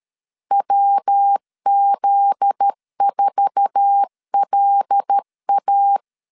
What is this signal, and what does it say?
Morse code, heard as a keyed tone, about 13 wpm, 800 Hz. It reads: WZ4LA
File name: LEZIONI / 8 YJ45 / 8 YJ45 12wpm 2151